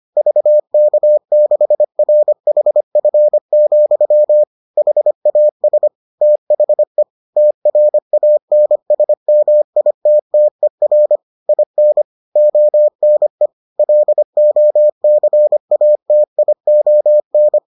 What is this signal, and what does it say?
Morse code, heard as a keyed tone, about 25 wpm, 600 Hz. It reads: VK6RHF, HAS THE TRANSMITTER IN ONE LOCATION